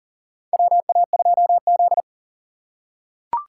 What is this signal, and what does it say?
Morse code, heard as a keyed tone, about 40 wpm, 700 Hz. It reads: WA2Z
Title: USA Call Signs - 4 Characters - 4x Character Spacing - 40wpm